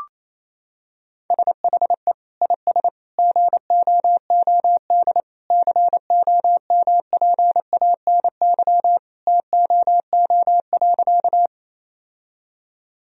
Morse code, heard as a keyed tone, about 28 wpm, 700 Hz. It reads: SHE IS GOOD COMPANY TOO.
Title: Sentences from Top 1000 Words - Review Part 3 - 28wpm